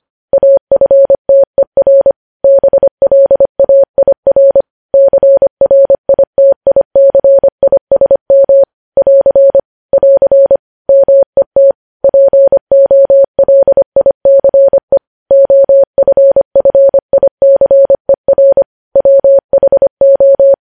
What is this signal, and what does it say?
Morse code, heard as a keyed tone, about 25 wpm, 550 Hz. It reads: AFTER BLAIR CRITICISM <AR> <AR> MET POLICE OFFICER WHO
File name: podcast / News headlines-1x-25x25